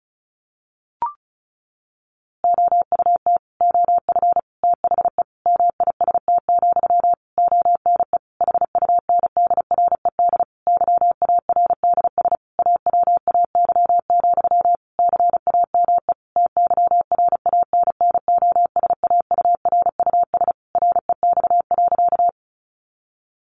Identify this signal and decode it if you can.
Morse code, heard as a keyed tone, about 34 wpm, 700 Hz. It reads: OUT OF THE MIST, ONE HUNDRED YARDS AWAY, CAME TYRANNOSAURUS REX.
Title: A Sound Of Thunder 35wpm